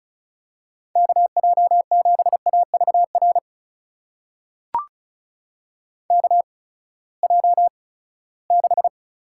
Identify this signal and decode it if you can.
Morse code, heard as a keyed tone, about 36 wpm, spaced as for 20 wpm, 700 Hz. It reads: KJ7AVR KJ6
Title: USA Call Signs - 6 Characters - 8x Character Spacing - 35wpm